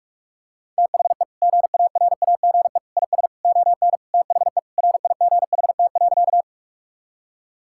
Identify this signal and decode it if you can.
Morse code, heard as a keyed tone, about 42 wpm, 700 Hz. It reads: THE GARAGE IS ON THE RIGHT.
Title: Sentences from Top 1600 Words 45wpm